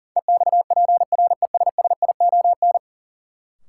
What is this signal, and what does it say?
Morse code, heard as a keyed tone, about 40 wpm, 700 Hz. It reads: EXPRESSION
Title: Sentences from Top 1200 Words - New Word Drill - 40wpm